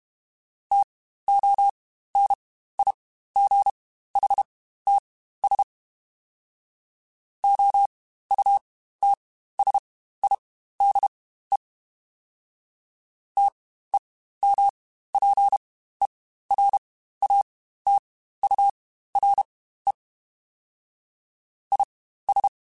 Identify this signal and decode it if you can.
Morse code, heard as a keyed tone, about 32 wpm, spaced as for 15 wpm, 800 Hz. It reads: TONIGHTS OUTSIDE TEMPERATURE IS
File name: Spaced Characters / Spaced Characters-mp3s / Weather 5 5 sp @30